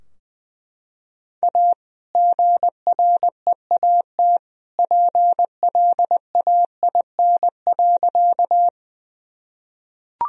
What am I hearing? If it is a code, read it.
Morse code, heard as a keyed tone, about 20 wpm, 700 Hz. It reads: A GREAT PLAIN.
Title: Sentences from Top 1000 Words - Review Part 2 - 20wpm